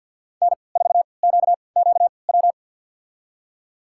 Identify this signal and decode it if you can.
Morse code, heard as a keyed tone, about 46 wpm, spaced as for 28 wpm, 700 Hz. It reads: N4QQW
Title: USA Call Signs - 5 Characters - 3x Character Spacing - 50wpm